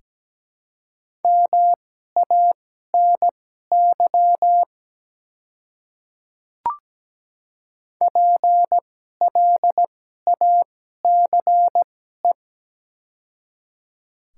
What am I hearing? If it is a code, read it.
Morse code, heard as a keyed tone, about 17 wpm, spaced as for 12 wpm, 700 Hz. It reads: MANY PLACE
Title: 2x Character Spacing - Top 200 Words - 17wpm